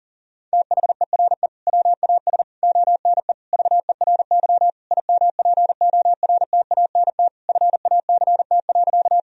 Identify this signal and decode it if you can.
Morse code, heard as a keyed tone, about 40 wpm, 700 Hz. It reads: THERE WAS ONE VERY IMPORTANT FACT.
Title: Sentences from Top 1000 Words - Review Part 3 - 40wpm